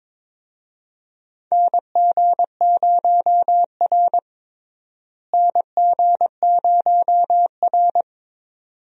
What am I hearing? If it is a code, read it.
Morse code, heard as a keyed tone, about 22 wpm, 700 Hz. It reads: NG0R NG0R
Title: USA Call Signs - 4 Characters - Repeated 2x - 22wpm